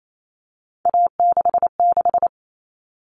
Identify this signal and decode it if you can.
Morse code, heard as a keyed tone, about 28 wpm, 700 Hz. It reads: A66
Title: All Cut Numbers - 3-Digit 28wpm